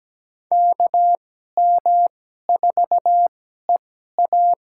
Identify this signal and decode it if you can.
Morse code, heard as a keyed tone, about 17 wpm, spaced as for 12 wpm, 700 Hz. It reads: KM4EA
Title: USA Call Signs - 2x Character Spacing - 17wpm